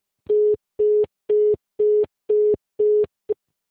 Congestion tone (425 Hz).
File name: out- -2004- - - .1473